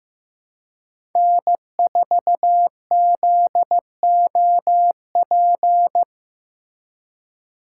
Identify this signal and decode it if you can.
Morse code, heard as a keyed tone, about 15 wpm, 700 Hz. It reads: N4ZOP
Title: USA Call Signs - 5 Characters - 3x Character Spacing - 15wpm